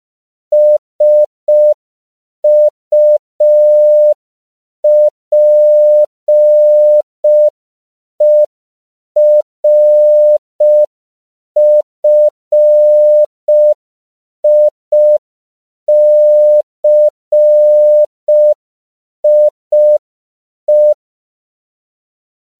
Morse code, heard as a keyed tone, about 5 wpm, 600 Hz. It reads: SUPERFICIE
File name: CW / 5PPM